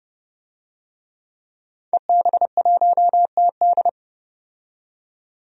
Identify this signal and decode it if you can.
Morse code, heard as a keyed tone, about 30 wpm, 700 Hz. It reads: EB1TD
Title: DXCC-100 Call Signs - 5 Characters - 30wpm